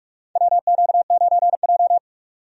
Morse code, heard as a keyed tone, about 46 wpm, 700 Hz. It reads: WQ9J
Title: USA Call Signs - 4 Characters - 45wpm